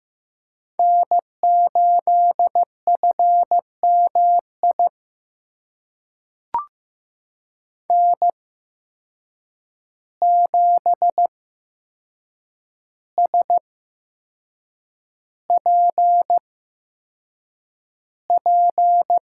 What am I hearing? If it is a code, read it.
Morse code, heard as a keyed tone, about 16 wpm, spaced as for 6 wpm, 700 Hz. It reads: N8FMI N7SPP
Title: USA Call Signs - 5 Characters - 8x Character Spacing - 15wpm